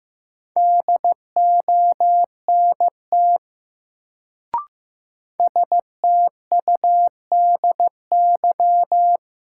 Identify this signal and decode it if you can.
Morse code, heard as a keyed tone, about 15 wpm, 700 Hz. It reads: DONT STUDY
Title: Sentences from Top 300 Words - New Word Drill - 15wpm